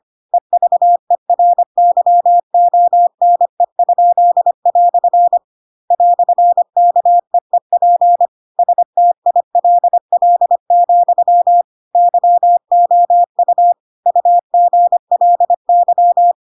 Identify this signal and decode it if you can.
Morse code, heard as a keyed tone, about 25 wpm, 700 Hz. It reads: EVERYONE?" "KEEP STILL, YOU UGLY
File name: Pinocchio / Pinocchio ch4 25 25 ik1hge